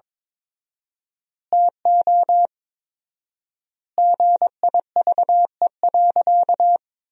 Morse code, heard as a keyed tone, about 22 wpm, 700 Hz. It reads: TO GIVE.
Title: 4x Word-Spacing - Sentences from Top 200 Words - 22wpm